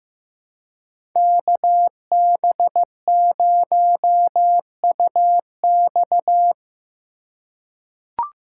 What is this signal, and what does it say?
Morse code, heard as a keyed tone, about 15 wpm, 700 Hz. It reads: KB0UX
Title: USA Call Signs - 5 Characters - 3x Character Spacing - 15wpm